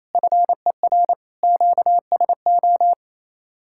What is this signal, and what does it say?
Morse code, heard as a keyed tone, about 37 wpm, 700 Hz. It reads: FER QSO